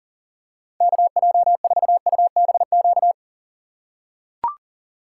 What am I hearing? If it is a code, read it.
Morse code, heard as a keyed tone, about 37 wpm, 700 Hz. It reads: KJ4UBQ
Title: USA Call Signs - 6 Characters - 6x Character Spacing - 40wpm